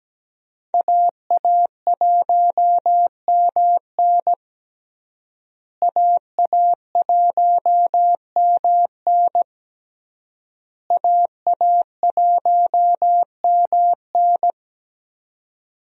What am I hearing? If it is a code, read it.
Morse code, heard as a keyed tone, about 17 wpm, 700 Hz. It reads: AA1MN AA1MN AA1MN
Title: USA Call Signs - 5 Characters - Repeated 3x - 17wpm